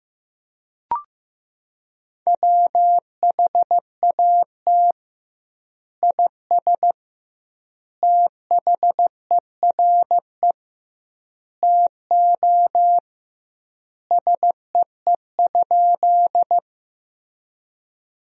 Morse code, heard as a keyed tone, about 15 wpm, 700 Hz. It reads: WHAT IS THERE TO SEE?
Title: 2x Word-Spacing - Sentences from Top 100 Words - 15wpm